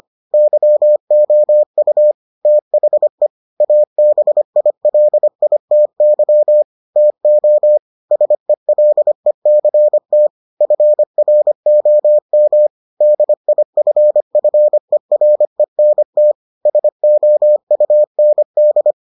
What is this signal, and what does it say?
Morse code, heard as a keyed tone, about 25 wpm, 600 Hz. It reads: YOU THE ABILITY TO SELECT FROM DIFFERENT SOUND